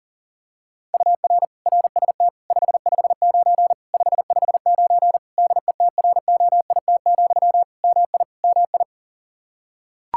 Morse code, heard as a keyed tone, about 40 wpm, 700 Hz. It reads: UR RST 559 559 DETROIT, MI MI
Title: Practice QSO - Exchange 1 - Beginning - 40wpm